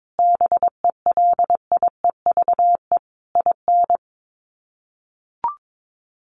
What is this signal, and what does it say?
Morse code, heard as a keyed tone, about 22 wpm, 700 Hz. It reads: BELIEVE IN
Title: Sets of 5 Words Encore 22wpm